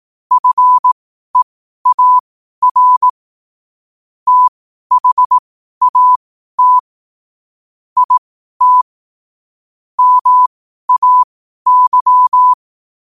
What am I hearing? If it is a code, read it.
Morse code, heard as a keyed tone, about 18 wpm, spaced as for 12 wpm, 1000 Hz. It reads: FEAR THAT IT MAY
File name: qotdfiles / 12wpm